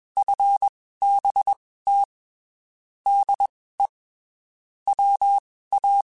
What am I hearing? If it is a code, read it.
Morse code, heard as a keyed tone, about 21 wpm, spaced as for 15 wpm, 800 Hz. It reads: FBT DE WA
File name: QSOs / QSOs-mp3s / WD0FBT 1@20